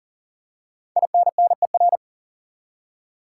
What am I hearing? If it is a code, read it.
Morse code, heard as a keyed tone, about 40 wpm, 700 Hz. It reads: INNER
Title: Five Letter Words - 40wpm